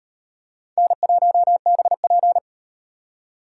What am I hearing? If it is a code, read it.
Morse code, heard as a keyed tone, about 34 wpm, 700 Hz. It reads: N1BP